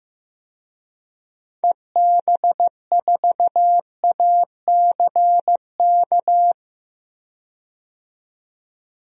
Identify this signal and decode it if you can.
Morse code, heard as a keyed tone, about 15 wpm, 700 Hz. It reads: EB4ACK
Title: DXCC-100 Call Signs - 6 Characters - 15wpm